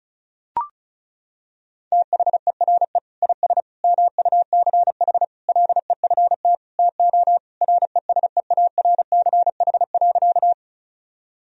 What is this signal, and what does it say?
Morse code, heard as a keyed tone, about 36 wpm, 700 Hz. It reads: THERE IS MUCH LEFT TO RESEARCH.
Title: Sentences from Top 1000 Words - Review Part 4 - 35wpm